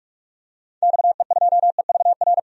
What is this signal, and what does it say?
Morse code, heard as a keyed tone, about 46 wpm, 700 Hz. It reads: XE2EVR
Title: DXCC-100 Call Signs - 6 Characters - 45wpm